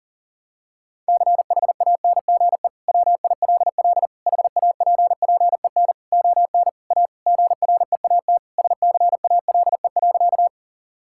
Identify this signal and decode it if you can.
Morse code, heard as a keyed tone, about 37 wpm, 700 Hz. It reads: CHANGE WILL HAPPEN ON A GREAT SCALE.
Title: Sentences from Top 1000 Words - Review Part 5 - 40wpm